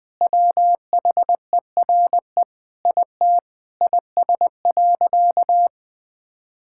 Morse code, heard as a keyed tone, about 20 wpm, 700 Hz. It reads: WHERE IT IS.
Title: Sentences with 5-Word Phrases - Spoken Beginning - 20wpm